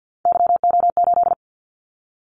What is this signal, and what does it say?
Morse code, heard as a keyed tone, about 50 wpm, 700 Hz. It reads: YO8
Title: Callsign Prefixes for Top 25 DXCC - 50wpm